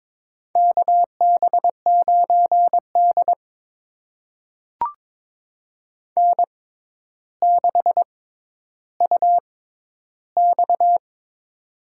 Morse code, heard as a keyed tone, about 22 wpm, spaced as for 11 wpm, 700 Hz. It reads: KB9D N6UX
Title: USA Call Signs - 4 Characters - 6x Character Spacing - 22wpm